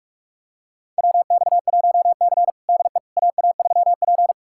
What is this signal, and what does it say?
Morse code, heard as a keyed tone, about 45 wpm, 700 Hz. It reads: WX1C DE AA3P
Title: USA Callsign DE Callsign - 4 Characters - 45wpm